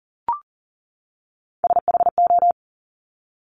Morse code, heard as a keyed tone, about 39 wpm, 690 Hz. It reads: SHO
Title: Top Three Letter Combinations in Words 40wpm